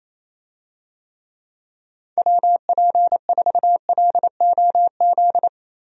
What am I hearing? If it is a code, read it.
Morse code, heard as a keyed tone, about 28 wpm, 700 Hz. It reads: WP4LOZ
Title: USA Call Signs - 6 Characters - 28wpm